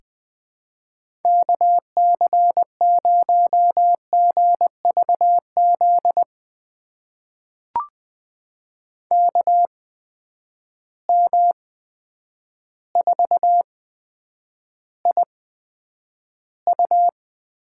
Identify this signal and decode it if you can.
Morse code, heard as a keyed tone, about 19 wpm, spaced as for 9 wpm, 700 Hz. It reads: KC0GVZ KM4IU
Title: USA Call Signs - 6 Characters - 8x Character Spacing - 20wpm